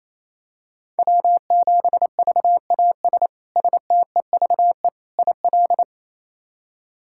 Morse code, heard as a keyed tone, about 28 wpm, 700 Hz. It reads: W7VAS STEVE IL
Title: K1USN SST QSO Practice - 28wpm